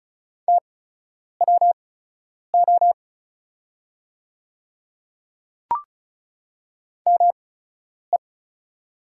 Morse code, heard as a keyed tone, about 36 wpm, spaced as for 10 wpm, 700 Hz. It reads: TWO ME
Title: 8x Character Spacing - Top 300 Words - 35wpm